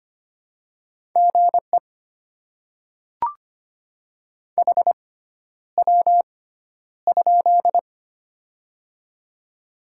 Morse code, heard as a keyed tone, about 25 wpm, spaced as for 11 wpm, 700 Hz. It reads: GE HW?
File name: Top 20 QSO Elements - 6x Character Spacing - 25wpm